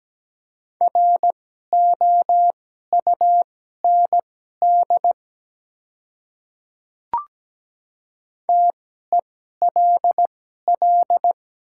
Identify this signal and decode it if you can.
Morse code, heard as a keyed tone, about 17 wpm, spaced as for 12 wpm, 700 Hz. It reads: ROUND TELL
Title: 2x Character Spacing - Top 200 Words - 17wpm